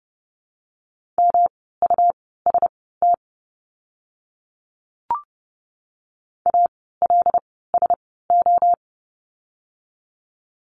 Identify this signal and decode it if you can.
Morse code, heard as a keyed tone, about 30 wpm, spaced as for 17 wpm, 700 Hz. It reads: MUST ALSO
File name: course / 30:10wpm / 237 Lesson 38f - Letter q - Words 30wpm